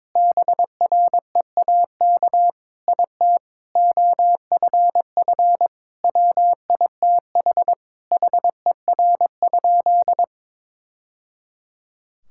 Morse code, heard as a keyed tone, about 22 wpm, 700 Hz. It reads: BREAK IT OFF WITH HER?
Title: Sentences from Top 1000 Words - Review Part 6 - 22wpm